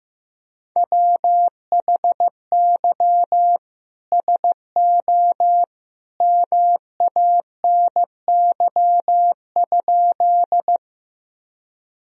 Morse code, heard as a keyed tone, about 16 wpm, 700 Hz. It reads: WHY SO MANY?
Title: Sentences from Top 1000 Words - Review Part 1 - 15wpm